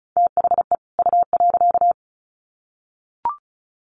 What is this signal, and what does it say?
Morse code, heard as a keyed tone, about 34 wpm, 700 Hz. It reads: THE U.
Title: Sets of 2 Words 35wpm